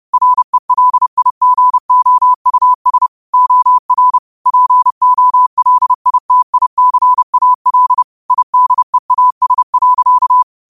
Morse code, heard as a keyed tone, about 30 wpm, 1000 Hz. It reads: RELIGOUS OR POLITICAL IDEAS.
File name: qotdfiles / 30wpm